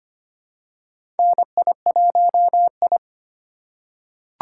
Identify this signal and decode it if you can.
Morse code, heard as a keyed tone, about 25 wpm, 700 Hz. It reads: NI1I
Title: USA Call Signs - 4 Characters - 4x Character Spacing - 25wpm